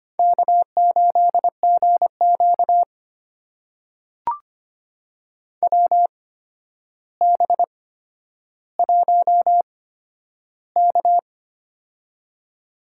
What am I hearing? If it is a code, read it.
Morse code, heard as a keyed tone, about 25 wpm, spaced as for 11 wpm, 700 Hz. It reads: K8GQ WB1K
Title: USA Call Signs - 4 Characters - 8x Character Spacing - 25wpm